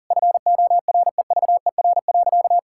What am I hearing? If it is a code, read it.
Morse code, heard as a keyed tone, about 40 wpm, 700 Hz. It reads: FOREVER.
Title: Sentences from Top 1600 Words 40wpm